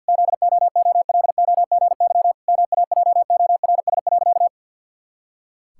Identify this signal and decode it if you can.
Morse code, heard as a keyed tone, about 51 wpm, 700 Hz. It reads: ZOOLOGY MAJORS.
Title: Pangram Challenge v2 - 50wpm